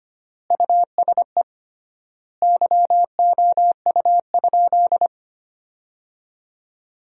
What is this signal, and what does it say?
Morse code, heard as a keyed tone, about 25 wpm, 700 Hz. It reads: USE YOU?
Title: 3x Word-Spacing - Sentences from Top 200 Words - 25wpm